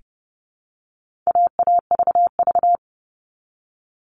Morse code, heard as a keyed tone, about 30 wpm, 700 Hz. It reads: AAVV
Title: All Cut Numbers - 4-Digit 30wpm